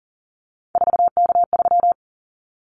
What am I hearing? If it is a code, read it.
Morse code, heard as a keyed tone, about 40 wpm, 700 Hz. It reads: 4K3
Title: Callsign Prefixes for Top 100 DXCC - 40wpm